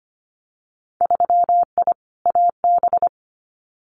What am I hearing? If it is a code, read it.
Morse code, heard as a keyed tone, about 25 wpm, 700 Hz. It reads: 3I AB
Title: Winter Field Day Exchanges Encore 25wpm